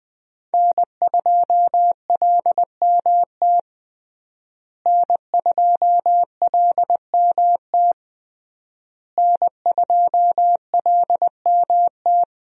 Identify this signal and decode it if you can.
Morse code, heard as a keyed tone, about 21 wpm, 700 Hz. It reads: N2LMT N2LMT N2LMT
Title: USA Call Signs - 5 Characters - Repeated 3x - 20wpm